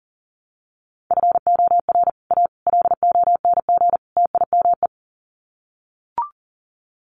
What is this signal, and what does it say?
Morse code, heard as a keyed tone, about 40 wpm, 700 Hz. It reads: FOR A LONG TIME
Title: Sets of 4 Words 40wpm